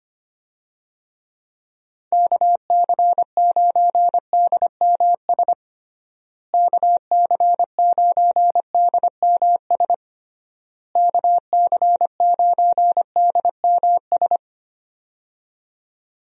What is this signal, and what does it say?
Morse code, heard as a keyed tone, about 25 wpm, 700 Hz. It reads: KC9DMS KC9DMS KC9DMS
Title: USA Call Signs - 6 Characters - Repeated 3x - 25wpm